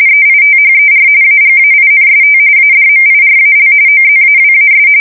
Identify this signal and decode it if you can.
RTTY, about 45.45 baud. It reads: EITHER AGAINST ADMIRAL OZAWAS
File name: fsk170 / battleshipnj